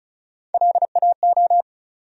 Morse code, heard as a keyed tone, about 36 wpm, 700 Hz. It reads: LAO
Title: USA Call Sign Postfixes - 3 Letters 35wpm